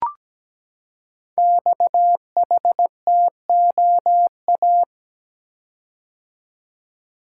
Morse code, heard as a keyed tone, about 17 wpm, 700 Hz. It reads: XHTOA
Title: Code Groups - 5 Letters - 17wpm